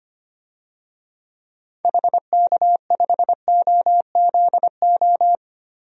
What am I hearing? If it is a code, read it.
Morse code, heard as a keyed tone, about 25 wpm, 700 Hz. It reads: HK5OZO
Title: DXCC-100 Call Signs - 6 Characters - 25wpm